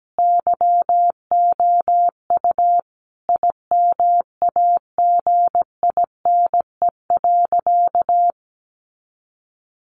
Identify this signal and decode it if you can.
Morse code, heard as a keyed tone, about 17 wpm, 700 Hz. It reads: YOU IMAGINE.
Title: Sentences from Top 1400 Words 17wpm